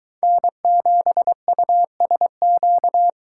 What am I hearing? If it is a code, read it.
Morse code, heard as a keyed tone, about 22 wpm, 700 Hz. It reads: N7USQ